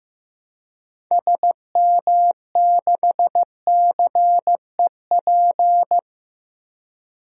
Morse code, heard as a keyed tone, about 15 wpm, 700 Hz. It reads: SM6CEP
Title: DXCC-100 Call Signs - 6 Characters - Repeated 3x - 15wpm